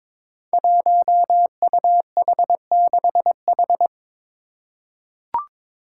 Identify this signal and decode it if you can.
Morse code, heard as a keyed tone, about 22 wpm, 700 Hz. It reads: 1UH6H
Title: Five Challenging Letter-Number Combinations 22wpm